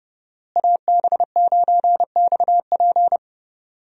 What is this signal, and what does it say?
Morse code, heard as a keyed tone, about 30 wpm, 700 Hz. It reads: AB9XP
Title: USA Call Signs - 5 Characters - Repeated 2x - 30wpm